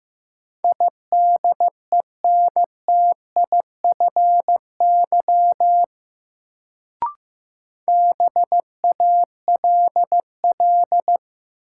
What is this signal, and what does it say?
Morse code, heard as a keyed tone, about 15 wpm, 700 Hz. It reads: IDENTIFY BALL
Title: Sentences from Top 900 Words - New Word Drill - 15wpm